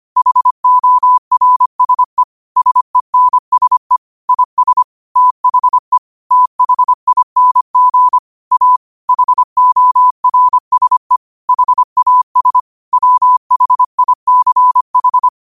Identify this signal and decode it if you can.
Morse code, heard as a keyed tone, about 25 wpm, 1000 Hz. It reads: HORSE SENSE IS THE THING A HORSE HAS WHICH